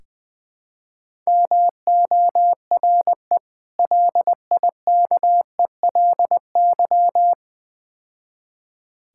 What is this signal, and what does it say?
Morse code, heard as a keyed tone, about 21 wpm, 700 Hz. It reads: MORE LIKELY
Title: Build A Phrase - 4-Word Phrases - 20wpm